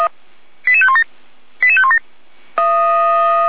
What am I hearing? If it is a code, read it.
Five-tone selective call (ZVEI), heard as quick runs of five tones, tones of about 70 ms. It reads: 88517 88517
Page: SDR/ZVEI